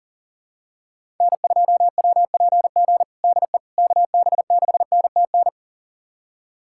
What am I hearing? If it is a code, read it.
Morse code, heard as a keyed tone, about 40 wpm, 700 Hz. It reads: N2WPG DE KB6NTN
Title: USA Callsign DE Callsign 40wpm